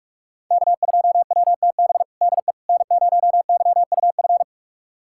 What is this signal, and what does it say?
Morse code, heard as a keyed tone, about 47 wpm, 700 Hz. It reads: K2WTB DE N0YUF
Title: USA Callsign DE Callsign - 5 Characters - 45wpm